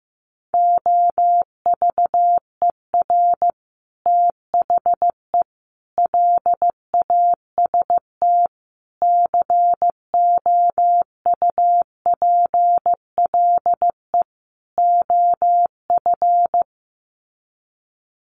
Morse code, heard as a keyed tone, about 15 wpm, 700 Hz. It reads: OVER THE LAST COUPLE OF
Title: Sets of 5 Words Encore 15wpm